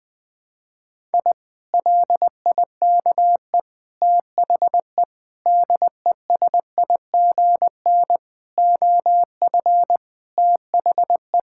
Morse code, heard as a keyed tone, about 20 wpm, 700 Hz. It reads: I LIKE THE DESIGN OF THE
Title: Sentences from Top 1000 Words - Review Part 5 - 20wpm